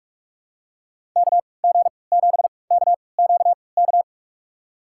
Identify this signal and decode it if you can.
Morse code, heard as a keyed tone, about 47 wpm, spaced as for 26 wpm, 700 Hz. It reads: KG7KQK
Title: USA Call Signs - 6 Characters - 3x Character Spacing - 45wpm